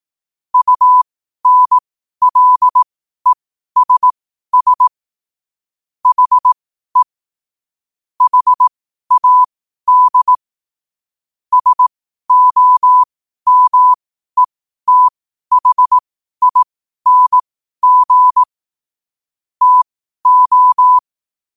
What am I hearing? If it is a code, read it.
Morse code, heard as a keyed tone, about 18 wpm, spaced as for 12 wpm, 1000 Hz. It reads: UNLESS HE HAD SOMETHING TO
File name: qotdfiles / 12wpm